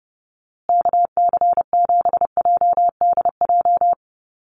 Morse code, heard as a keyed tone, about 30 wpm, 700 Hz. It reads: KC7JDJ
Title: USA Call Signs Repeated 2x 30wpm